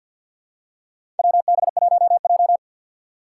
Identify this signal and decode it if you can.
Morse code, heard as a keyed tone, about 52 wpm, 700 Hz. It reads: WB1J
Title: USA Call Signs - 4 Characters - 2x Character Spacing - 50wpm